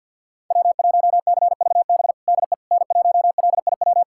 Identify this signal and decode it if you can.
Morse code, heard as a keyed tone, about 51 wpm, 700 Hz. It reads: W1CVB DE N1LIW
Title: USA Callsign DE Callsign - 5 Characters - 50wpm